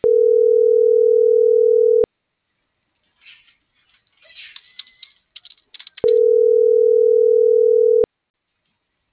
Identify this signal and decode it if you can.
Ringback tone (North America), 440 + 480 Hz.